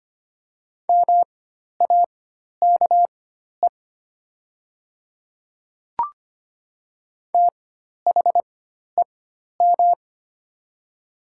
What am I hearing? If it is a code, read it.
Morse code, heard as a keyed tone, about 25 wpm, spaced as for 12 wpm, 700 Hz. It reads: MAKE THEM
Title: 4x Character Spacing - Top 100 Words - 25wpm